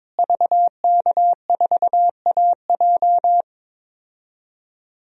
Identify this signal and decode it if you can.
Morse code, heard as a keyed tone, about 22 wpm, 700 Hz. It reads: VK4AJ
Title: DXCC-100 Call Signs - 5 Characters - Repeated 2x - 22wpm